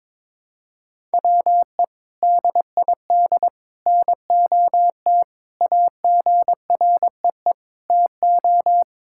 Morse code, heard as a keyed tone, about 22 wpm, 700 Hz. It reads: WE DID NOT AGREE TO
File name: practice / 22wpm / Sentences with 3-Word Phrases - 22wpm